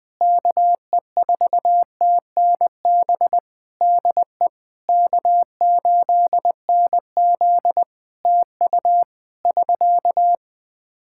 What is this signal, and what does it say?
Morse code, heard as a keyed tone, about 20 wpm, 700 Hz. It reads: KE4TNB DE K8NZ TU <SK>
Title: Practice QSO - Exchange 4 - Ending - 20wpm